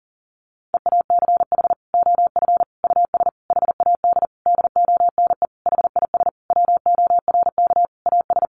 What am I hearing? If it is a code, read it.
Morse code, heard as a keyed tone, about 40 wpm, 700 Hz. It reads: EACH OF US HAD DONE HIS WORK AS
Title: Sentences from Top 500 Words 40wpm